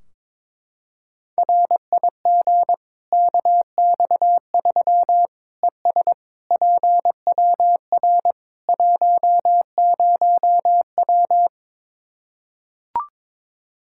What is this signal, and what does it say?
Morse code, heard as a keyed tone, about 22 wpm, 700 Hz. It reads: RIG KX3 ES PWR 10W
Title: Practice QSO - Exchange 2 - Info - 22wpm